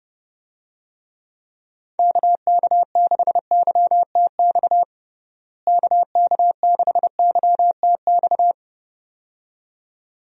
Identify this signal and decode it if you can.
Morse code, heard as a keyed tone, about 30 wpm, 700 Hz. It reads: KK6YTX KK6YTX
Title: USA Call Signs - 6 Characters - Repeated 2x - 30wpm